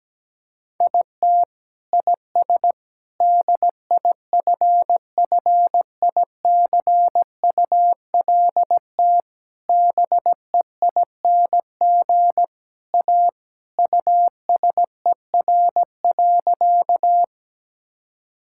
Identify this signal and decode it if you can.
Morse code, heard as a keyed tone, about 17 wpm, 700 Hz. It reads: IT IS DIFFICULT BEING A USER.
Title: Sentences from Top 1000 Words - Review Part 5 - 17wpm